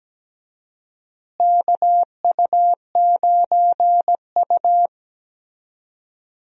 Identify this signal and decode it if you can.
Morse code, heard as a keyed tone, about 17 wpm, 700 Hz. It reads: KU9U
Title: USA Call Signs - 4 Characters - Repeated 3x - 17wpm